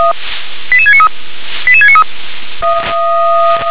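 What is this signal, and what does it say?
Five-tone selective call (ZVEI), heard as quick runs of five tones, tones of about 70 ms. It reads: 88693 88693
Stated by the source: Five-tone ZVEI selective call 88788, received by SDR.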